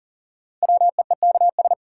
Morse code, heard as a keyed tone, about 40 wpm, 700 Hz. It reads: WEEKS
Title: Sets of 5 Words - New Word Drill - 40wpm